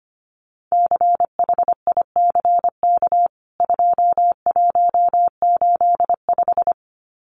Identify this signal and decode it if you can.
Morse code, heard as a keyed tone, about 25 wpm, 700 Hz. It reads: CHICK 2185
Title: CWT Exchange 25wpm